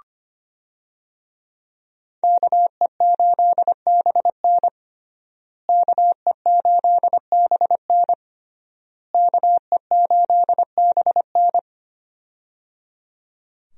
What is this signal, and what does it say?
Morse code, heard as a keyed tone, about 25 wpm, 700 Hz. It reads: KE8BN KE8BN KE8BN
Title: USA Call Signs - 5 Characters - Repeated 3x - 25wpm